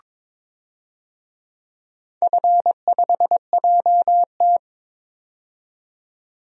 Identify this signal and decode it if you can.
Morse code, heard as a keyed tone, about 22 wpm, 700 Hz. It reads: F5JT
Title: DXCC-100 Call Signs - 4 Characters - 22wpm